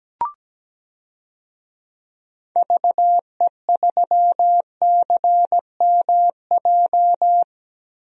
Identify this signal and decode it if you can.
Morse code, heard as a keyed tone, about 17 wpm, 700 Hz. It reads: VE3CMJ
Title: DXCC-100 Call Signs - 6 Characters - 17wpm